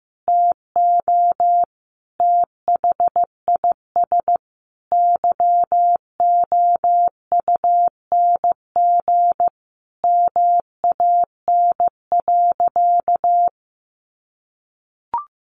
Morse code, heard as a keyed tone, about 15 wpm, 700 Hz. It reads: TO THIS YOUNG MAN.
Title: Sentences from Top 1300 Words 15wpm